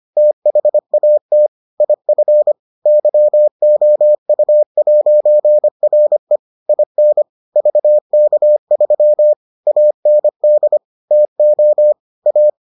Morse code, heard as a keyed tone, about 25 wpm, 600 Hz. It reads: THAT IF YOU'RE IN VK3 AND TO A